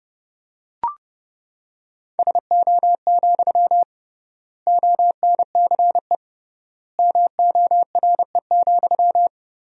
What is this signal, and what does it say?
Morse code, heard as a keyed tone, about 30 wpm, 700 Hz. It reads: SO, ONCE MORE,
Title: 3x Word-Spacing - Sentences from Top 300 Words - 30wpm